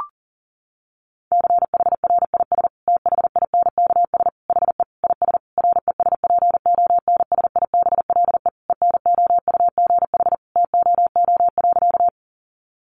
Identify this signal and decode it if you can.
Morse code, heard as a keyed tone, about 40 wpm, 700 Hz. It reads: CHRIS THINKS HE IS RESPONSIBLE ENOUGH TOO.
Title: Easy Phrases 40wpm